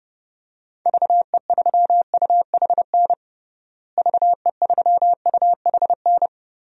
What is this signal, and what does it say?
Morse code, heard as a keyed tone, about 30 wpm, 700 Hz. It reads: VE3UHN VE3UHN
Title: DXCC-100 Call Signs - 6 Characters - Repeated 2x - 30wpm